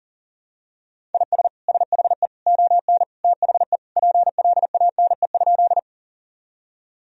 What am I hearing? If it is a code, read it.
Morse code, heard as a keyed tone, about 40 wpm, 700 Hz. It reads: IS SHE ON THE PLANE?
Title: Sentences from Top 1000 Words - Review Part 3 - 40wpm